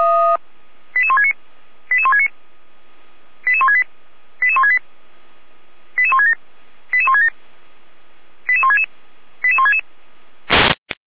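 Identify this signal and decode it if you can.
Five-tone selective call (ZVEI), heard as quick runs of five tones, tones of about 70 ms. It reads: 88169 88169 88168 88168 88167 88167 88166 88166
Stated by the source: Received by SDR, ZVEI five-tone call.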